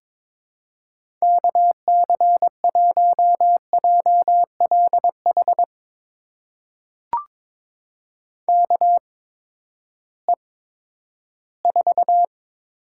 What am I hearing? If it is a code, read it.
Morse code, heard as a keyed tone, about 22 wpm, spaced as for 13 wpm, 700 Hz. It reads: KC1JLH KE4
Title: USA Call Signs - 6 Characters - 8x Character Spacing - 22wpm